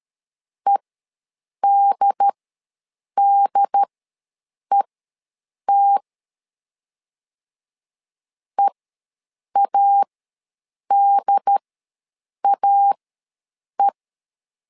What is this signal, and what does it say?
Morse code, heard as a keyed tone, about 13 wpm, spaced as for 7 wpm, 800 Hz. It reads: EDDET EADAE
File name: LEZIONI / 2 UHDN / 2 UHD ws 12wpm 1722